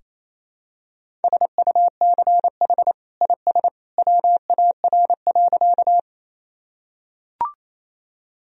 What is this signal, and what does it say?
Morse code, heard as a keyed tone, about 28 wpm, 700 Hz. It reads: SUCH IS WAR.
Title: Sentences from Top 1000 Words - Review Part 2 - 28wpm